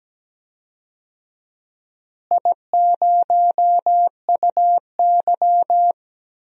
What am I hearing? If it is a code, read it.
Morse code, heard as a keyed tone, about 17 wpm, 700 Hz. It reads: I0UY
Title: DXCC-100 Call Signs - 4 Characters - Repeated 3x - 17wpm